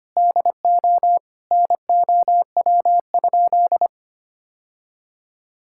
Morse code, heard as a keyed tone, about 25 wpm, 700 Hz. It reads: DO NOW?
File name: practice / 25wpm / Sentences with 2-Word Phrases - 25wpm